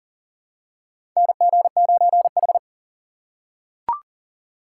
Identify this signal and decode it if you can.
Morse code, heard as a keyed tone, about 40 wpm, 700 Hz. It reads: NG9H
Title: USA Call Signs - 3x Character Spacing - 40wpm